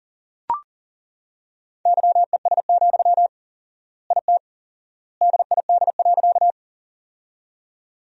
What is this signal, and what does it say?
Morse code, heard as a keyed tone, about 40 wpm, 700 Hz. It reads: YES, IT DID.
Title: 4x Word-Spacing - Sentences from Top 500 Words - 40wpm